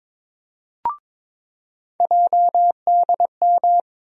Morse code, heard as a keyed tone, about 22 wpm, 700 Hz. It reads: JDM
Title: USA Call Sign Postfixes - 3 Letters 22wpm